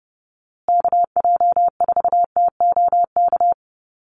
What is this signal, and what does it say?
Morse code, heard as a keyed tone, about 30 wpm, 700 Hz. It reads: KJ4TOK
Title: USA Call Signs Repeated 3x 30wpm